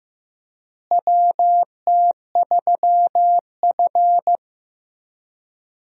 Morse code, heard as a keyed tone, about 15 wpm, 700 Hz. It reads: WT3F
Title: USA Call Signs - 4 Characters - Repeated 3x - 15wpm